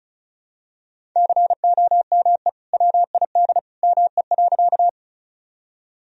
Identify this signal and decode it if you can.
Morse code, heard as a keyed tone, about 34 wpm, 700 Hz. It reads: COME WID ME.
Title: Sentences from Top 200 Words and Abbreviations 35wpm